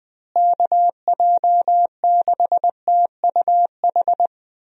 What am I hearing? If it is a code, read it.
Morse code, heard as a keyed tone, about 21 wpm, 700 Hz. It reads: KJ6TUH
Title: USA Call Signs - 6 Characters - Repeated 3x - 20wpm